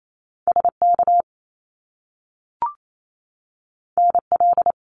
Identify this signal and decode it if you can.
Morse code, heard as a keyed tone, about 28 wpm, 700 Hz. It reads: SK NL
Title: Canadian Provinces and Territories 28wpm